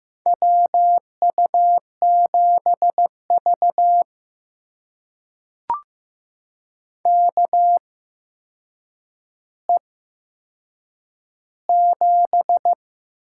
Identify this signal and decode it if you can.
Morse code, heard as a keyed tone, about 15 wpm, spaced as for 7 wpm, 700 Hz. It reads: WU7V KE7
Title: USA Call Signs - 8x Character Spacing - 15wpm